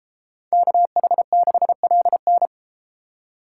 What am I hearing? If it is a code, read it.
Morse code, heard as a keyed tone, about 33 wpm, 700 Hz. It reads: KH6LN